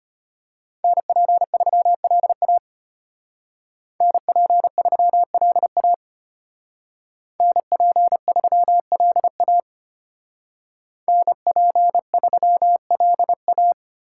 Morse code, heard as a keyed tone, about 31 wpm, 700 Hz. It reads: NP3LA NP3LA NP3LA NP3LA
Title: Speed Racing - USA Call Signs 25wpm